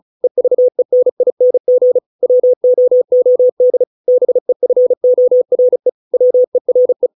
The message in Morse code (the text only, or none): EVENING WOOD BEFORE WERE